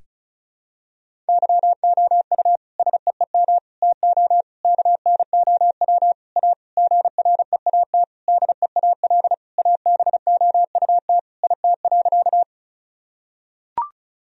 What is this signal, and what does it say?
Morse code, heard as a keyed tone, about 34 wpm, 690 Hz. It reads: YOU SEEM TO KNOW A GREAT DEAL ABOUT IT.
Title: Sentences from Top 1000 Words - Review Part 1 - 35wpm